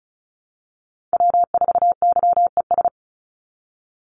Morse code, heard as a keyed tone, about 34 wpm, 700 Hz. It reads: W4YES
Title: Call Signs Repeated 3x - 35wpm